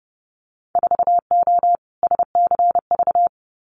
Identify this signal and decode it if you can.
Morse code, heard as a keyed tone, about 30 wpm, 700 Hz. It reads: 4O SCV
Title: Winter Field Day Exchanges 30wpm